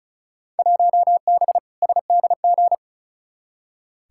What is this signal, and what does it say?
Morse code, heard as a keyed tone, about 36 wpm, 700 Hz. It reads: 1B SDG
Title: Field Day Exchanges - 35wpm